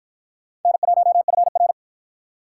Morse code, heard as a keyed tone, about 53 wpm, 700 Hz. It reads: N1FR